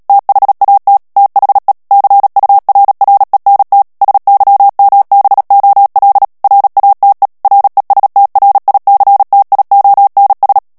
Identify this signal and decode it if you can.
Morse code, heard as a keyed tone, about 37 wpm, 800 Hz. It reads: THAT THE CURRENT SYMBOL RATE RESTRICTIONS
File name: morse mp3 / ANGLAIS texte clair 3 vitesse 35 mots minute